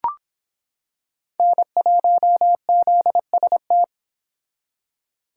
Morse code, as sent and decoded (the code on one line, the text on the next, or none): -. .---- --.. ... -
N1ZST